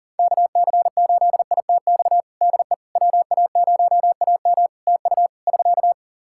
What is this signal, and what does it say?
Morse code, heard as a keyed tone, about 40 wpm, 700 Hz. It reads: KC8ITX DE WA0AM TU <SK>
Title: Practice QSO - Exchange 4 - Ending - 40wpm